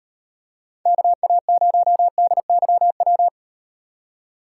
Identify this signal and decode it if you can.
Morse code, heard as a keyed tone, about 38 wpm, 700 Hz. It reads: KA0DYW